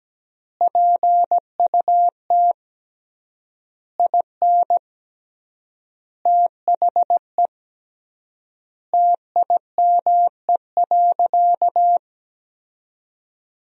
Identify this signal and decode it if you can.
Morse code, heard as a keyed tone, about 17 wpm, 700 Hz. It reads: PUT IN THE TIME.
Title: 3x Word-Spacing - Sentences from Top 200 Words - 17wpm